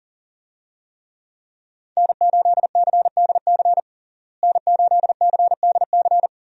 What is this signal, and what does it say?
Morse code, heard as a keyed tone, about 40 wpm, 700 Hz. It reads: N8CDC N8CDC
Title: USA Call Signs - 5 Characters - Repeated 2x - 40wpm